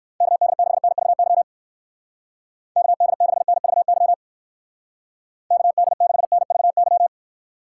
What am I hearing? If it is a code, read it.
Morse code, heard as a keyed tone, about 54 wpm, 700 Hz. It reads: KD6NVY KD6NVY KD6NVY
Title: Speed Racing - USA 2x3 Call Signs 45wpm